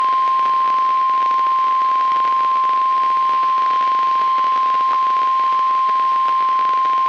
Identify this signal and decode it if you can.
Emergency Alert System (SAME) tones (the 1050 Hz attention tone). It is heard from 0.5 to 5.9 s.